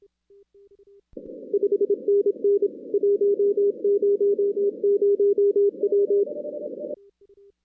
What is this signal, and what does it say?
Morse code, heard as a keyed tone, about 25 wpm, 400 Hz. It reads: 5NN 100W
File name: 2016 arrl dx cw / 2016-ARRL-DX-CW-K5TR- - 0-46